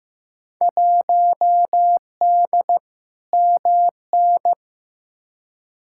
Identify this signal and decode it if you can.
Morse code, heard as a keyed tone, about 15 wpm, 700 Hz. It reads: 1D MN
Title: Field Day Exchanges - 15wpm